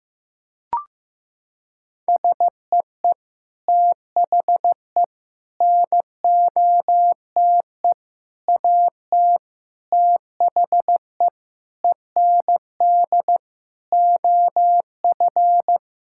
Morse code, heard as a keyed tone, about 15 wpm, 700 Hz. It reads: SEE THE NOTE AT THE END OF
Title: Sentences from Top 1000 Words - Review Part 5 - 15wpm